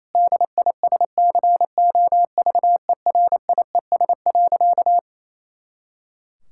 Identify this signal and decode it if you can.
Morse code, heard as a keyed tone, about 28 wpm, 700 Hz. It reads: DISCOVERIES.
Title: Pangram Challenge v2 - 28wpm